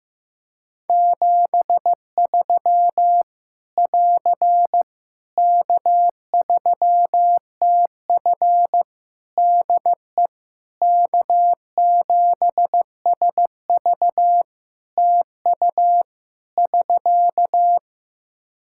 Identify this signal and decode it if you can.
Morse code, heard as a keyed tone, about 16 wpm, 700 Hz. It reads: 73 <AR> K3TF DE K7SV TU <SK>
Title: Practice QSO - Exchange 4 - Ending - 15wpm